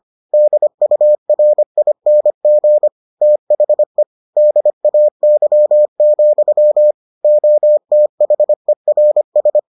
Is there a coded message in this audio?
Yes: Morse code from 0.3 to 9.6 s, heard as a keyed tone, about 25 wpm, 600 Hz. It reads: DURING THE DAY, OTHERS